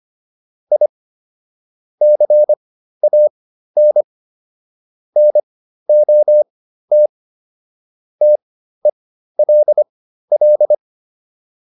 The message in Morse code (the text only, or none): I CAN NOT TELL